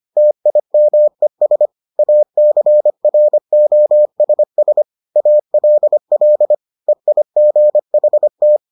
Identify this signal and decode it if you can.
Morse code, heard as a keyed tone, about 25 wpm, 600 Hz. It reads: TIMES ACROSS ALL EIGHT